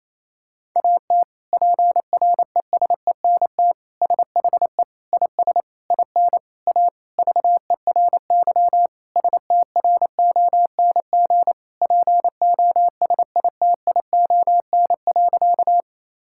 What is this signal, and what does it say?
Morse code, heard as a keyed tone, about 28 wpm, 700 Hz. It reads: AT PRESENT SHE IS IN A VERY STRONG POSITION.
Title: Sentences from Top 1000 Words - Review Part 5 - 28wpm